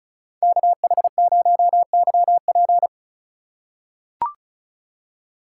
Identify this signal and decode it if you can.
Morse code, heard as a keyed tone, about 34 wpm, 700 Hz. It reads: KH0YP
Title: USA Call Signs - 5 Characters - 4x Character Spacing - 35wpm